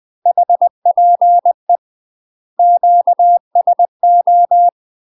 Morse code, heard as a keyed tone, about 20 wpm, 700 Hz. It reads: HPE QSO
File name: UK MASTER QSOs TRAINING / 20WPM / UK MASTER QSOs 0014